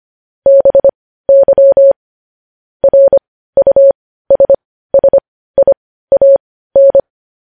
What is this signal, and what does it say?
Morse code, heard as a keyed tone, about 25 wpm, spaced as for 15 wpm, 550 Hz. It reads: BY RUSSIAN
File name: podcast / News headlines-1x-25x15